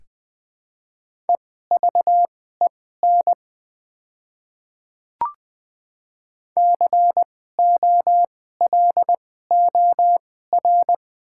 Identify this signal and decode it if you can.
Morse code, heard as a keyed tone, about 21 wpm, spaced as for 15 wpm, 700 Hz. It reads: EVEN COLOR
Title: 2x Character Spacing - Top 300 Words - 20wpm